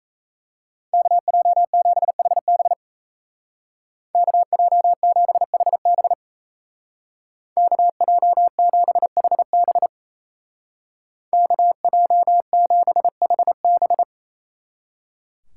Morse code, heard as a keyed tone, about 34 wpm, 700 Hz. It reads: KJ7HB KJ7HB KJ7HB KJ7HB
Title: Speed Racing - USA 2x2 Call Signs 28wpm